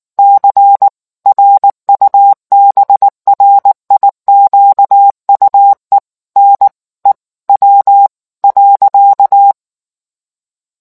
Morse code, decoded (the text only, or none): C RUBRIQUE NEW.